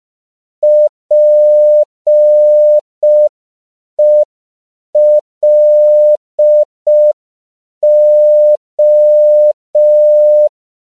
Morse code, heard as a keyed tone, about 5 wpm, 600 Hz. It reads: PELO
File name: CW / 5PPM